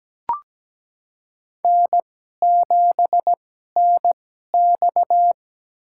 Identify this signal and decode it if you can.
Morse code, heard as a keyed tone, about 17 wpm, spaced as for 12 wpm, 700 Hz. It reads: N7NX
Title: USA Call Signs - 2x Character Spacing - 17wpm